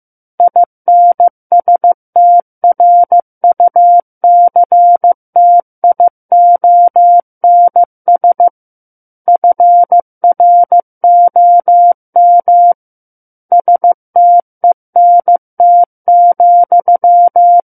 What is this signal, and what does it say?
Morse code, heard as a keyed tone, about 15 wpm, 700 Hz. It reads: INSTRUCTIONS FROM STENT,
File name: Book One / 06-Chapter 6 - 15 WPM